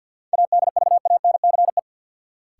Morse code, heard as a keyed tone, about 53 wpm, 700 Hz. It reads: ADVANCE
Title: Sentences from Top 1400 Words - New Word Drill - 50wpm